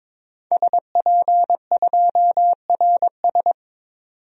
Morse code, heard as a keyed tone, about 21 wpm, 700 Hz. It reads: SP2RS
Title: DXCC-100 Call Signs - 5 Characters - 22wpm